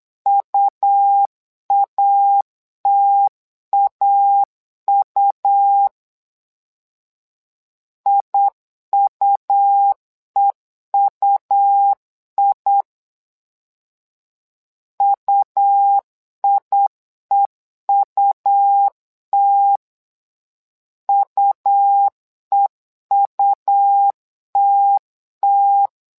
Morse code, heard as a keyed tone, about 8 wpm, 800 Hz. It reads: UATAU IUEUI UIEUT UEUTT
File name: LEZIONI / 2 UHDN / 2 U 8wpm 2028